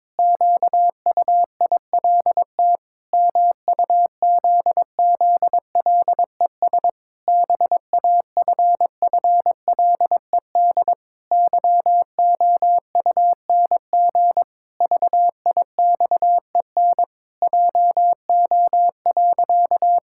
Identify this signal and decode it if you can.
Morse code, heard as a keyed tone, about 22 wpm, 700 Hz. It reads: QUILT MUZZLES BAFFLED YOUNG VIXEN JO.
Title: Pangram Challenge v2 - 22wpm